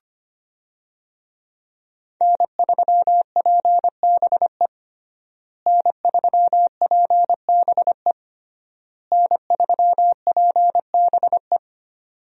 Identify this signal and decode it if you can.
Morse code, heard as a keyed tone, about 25 wpm, 700 Hz. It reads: N3PBE N3PBE N3PBE
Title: USA Call Signs - 5 Characters - Repeated 3x - 25wpm